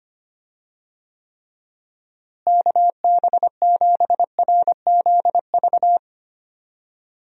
Morse code, heard as a keyed tone, about 25 wpm, 700 Hz. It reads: KB7RZV
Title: USA Call Signs - 6 Characters - Repeated 2x - 25wpm